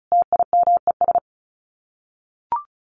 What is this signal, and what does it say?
Morse code, heard as a keyed tone, about 34 wpm, 700 Hz. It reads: TIMES
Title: Sets of 5 Words 35wpm